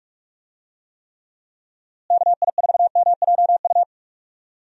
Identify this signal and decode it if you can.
Morse code, heard as a keyed tone, about 44 wpm, 700 Hz. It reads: KI4MJU
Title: USA Call Signs - 6 Characters - Repeated 3x - 45wpm